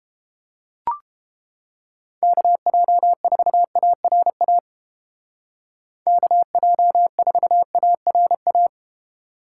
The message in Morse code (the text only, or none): KJ4ARA KJ4ARA